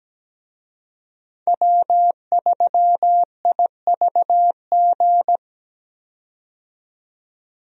Morse code, heard as a keyed tone, about 17 wpm, 700 Hz. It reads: W3IVG
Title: USA Call Signs - 5 Characters - 17wpm